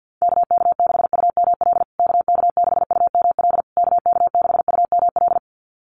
Morse code, heard as a keyed tone, about 48 wpm, 700 Hz. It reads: KK6UML KK6UML KK6UML
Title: USA Call Signs Repeated 3x 50wpm